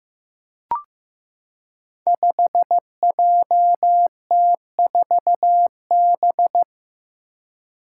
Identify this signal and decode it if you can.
Morse code, heard as a keyed tone, about 15 wpm, 700 Hz. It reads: 5JT4B
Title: Five Challenging Letter-Number Combinations 15wpm